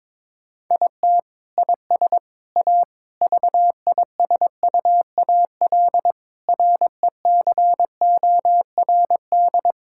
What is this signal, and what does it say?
Morse code, heard as a keyed tone, about 21 wpm, 700 Hz. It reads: IT IS A VISUAL RECORD
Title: Sentences from Top 1000 Words - Review Part 6 - 22wpm